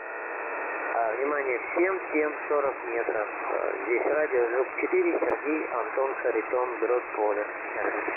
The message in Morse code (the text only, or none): none